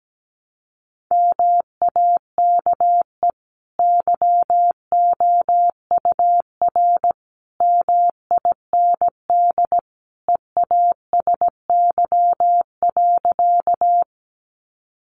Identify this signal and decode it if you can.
Morse code, heard as a keyed tone, about 17 wpm, 700 Hz. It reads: MAKE YOUR MIND EASY.
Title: Sentences from Top 700 Words 17wpm